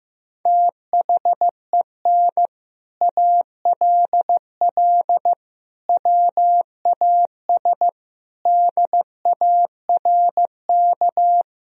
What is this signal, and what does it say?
Morse code, heard as a keyed tone, about 16 wpm, 700 Hz. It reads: THEN ALL WAS DARK
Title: Sentences from Top 500 Words and Abbreviations 15wpm